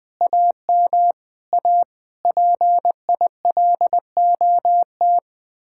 Morse code, heard as a keyed tone, about 20 wpm, 700 Hz. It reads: AM A PILOT
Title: Speed-Racing Common QSO Phrases Encore - 20wpm